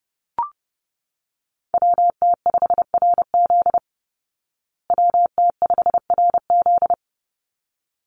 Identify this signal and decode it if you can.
Morse code, heard as a keyed tone, about 30 wpm, 700 Hz. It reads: WT5RZ WT5RZ
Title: Call Signs Repeated 3x - 30wpm